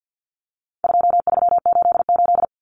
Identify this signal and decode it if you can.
Morse code, heard as a keyed tone, about 50 wpm, 700 Hz. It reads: 2388
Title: Numbers - 4-Digit 50wpm